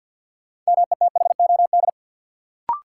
Morse code, heard as a keyed tone, about 53 wpm, 700 Hz. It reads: METHOD
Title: Sentences from Top 600 Words - New Word Drill - 50wpm